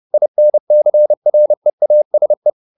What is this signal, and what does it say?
Morse code, heard as a keyed tone, about 30 wpm, 600 Hz. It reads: INCREASE